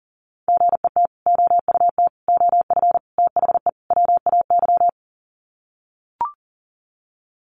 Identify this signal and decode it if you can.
Morse code, heard as a keyed tone, about 37 wpm, 700 Hz. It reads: GET OUT OF THE WAY
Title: Sets of 5 Words Encore 40wpm